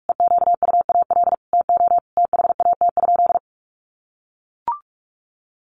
Morse code, heard as a keyed tone, about 44 wpm, 700 Hz. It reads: EQUAL TO THAT?
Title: Sentences from Top 1400 Words 45wpm